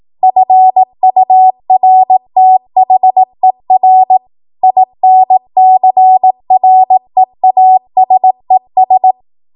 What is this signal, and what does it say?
Morse code, heard as a keyed tone, about 18 wpm, 750 Hz. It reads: FURTHER INCREASES
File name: Archive / 18 WPM / 18WPM